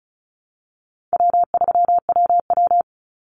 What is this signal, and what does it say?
Morse code, heard as a keyed tone, about 34 wpm, 700 Hz. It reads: W3WW
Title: Call Signs Repeated 2x - 35wpm